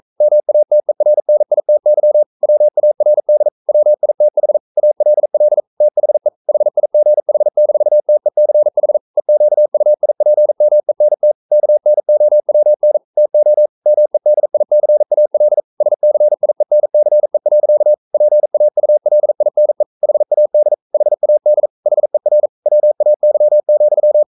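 Morse code, heard as a keyed tone, about 42 wpm, 600 Hz. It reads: MATERNITY WARD WITH ALL THE HIGH-TECH EQUIPMENT KNOWN TO MEDICAL SCIENCE. PAULINE HAD HAD HER WAY,
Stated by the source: Morse code sending of an English novel at 42 wpm.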